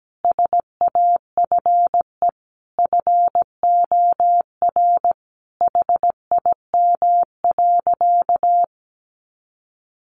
Morse code, heard as a keyed tone, about 17 wpm, 700 Hz. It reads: SAFE FOR HIM.
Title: Sentences from Top 900 Words 17wpm